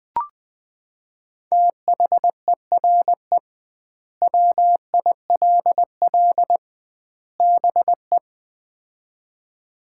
Morse code, heard as a keyed tone, about 20 wpm, 700 Hz. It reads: THERE WILL BE
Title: 2x Word-Spacing - Sets of 3 Words - 20wpm